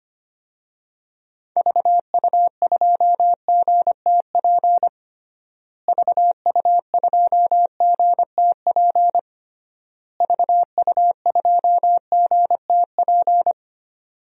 Morse code, heard as a keyed tone, about 25 wpm, 700 Hz. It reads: VU2GTP VU2GTP VU2GTP
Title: DXCC-100 Call Signs - 6 Characters - Repeated 3x - 25wpm